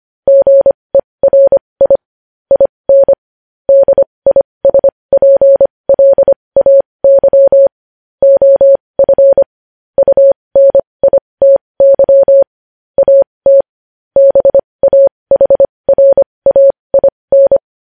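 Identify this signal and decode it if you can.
Morse code, heard as a keyed tone, about 25 wpm, spaced as for 20 wpm, 550 Hz. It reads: GERI IN DISPLAY OF UNITY AT BAHRAIN
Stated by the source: News headlines sent in Morse code at 20.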